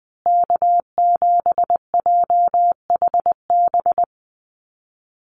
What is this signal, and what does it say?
Morse code, heard as a keyed tone, about 20 wpm, 700 Hz. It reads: K7JHB
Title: USA Call Signs Repeated 2x 20wpm